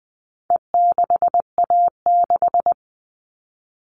Morse code, heard as a keyed tone, about 20 wpm, 700 Hz. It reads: E6A6
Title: All Cut Numbers - 4-Digit 20wpm